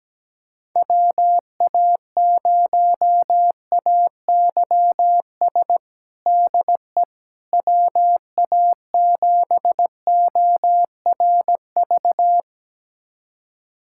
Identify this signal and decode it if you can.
Morse code, heard as a keyed tone, about 17 wpm, 700 Hz. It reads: WA0AYS DE WA7ORV
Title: USA Callsign DE Callsign - 6 Characters - 17wpm